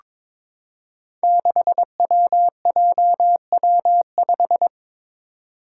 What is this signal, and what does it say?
Morse code, heard as a keyed tone, about 23 wpm, 700 Hz. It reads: 6WJW5
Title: Five Challenging Letter-Number Combinations 22wpm